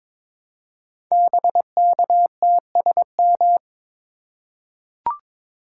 Morse code, heard as a keyed tone, about 22 wpm, 700 Hz. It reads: BKTSM
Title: Code Groups - 5 Letters - 22wpm